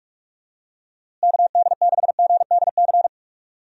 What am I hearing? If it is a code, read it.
Morse code, heard as a keyed tone, about 42 wpm, 700 Hz. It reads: KD6GDC
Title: USA Call Signs - 6 Characters - Repeated 2x - 45wpm